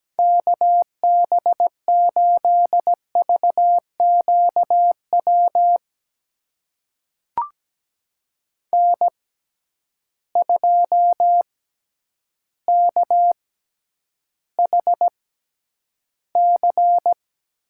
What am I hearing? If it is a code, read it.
Morse code, heard as a keyed tone, about 17 wpm, spaced as for 9 wpm, 700 Hz. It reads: KB8VQW N2KHC
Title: USA Call Signs - 6x Character Spacing - 17wpm